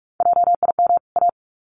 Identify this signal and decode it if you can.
Morse code, heard as a keyed tone, about 44 wpm, 700 Hz. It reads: JIM A
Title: CWT Exchange 45wpm